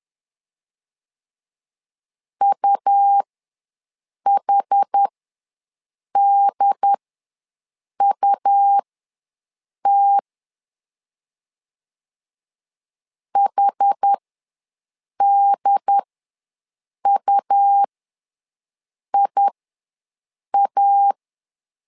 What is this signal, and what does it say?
Morse code, heard as a keyed tone, about 10 wpm, spaced as for 6 wpm, 800 Hz. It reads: UHDUT HDUIA
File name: LEZIONI / 2 UHDN / 2 UHD ws 10wpm 2058